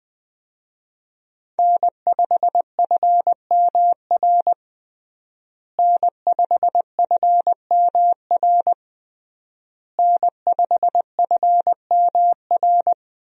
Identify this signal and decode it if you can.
Morse code, heard as a keyed tone, about 21 wpm, 700 Hz. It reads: N5FMR N5FMR N5FMR
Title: USA Call Signs - 5 Characters - Repeated 3x - 20wpm